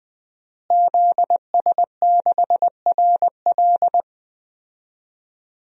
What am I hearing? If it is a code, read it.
Morse code, heard as a keyed tone, about 20 wpm, 700 Hz. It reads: ZS6RL
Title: DXCC-100 Call Signs - 5 Characters - Repeated 3x - 20wpm